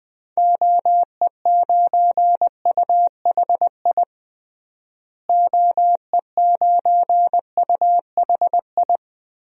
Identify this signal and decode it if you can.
Morse code, heard as a keyed tone, about 20 wpm, 700 Hz. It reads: OE9UHI OE9UHI
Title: DXCC-100 Call Signs - 6 Characters - Repeated 2x - 20wpm